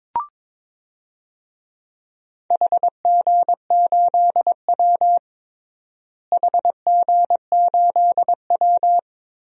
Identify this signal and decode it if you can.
Morse code, heard as a keyed tone, about 22 wpm, 700 Hz. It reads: HG8W HG8W
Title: DXCC-100 Call Signs - 4 Characters - Repeated 3x - 22wpm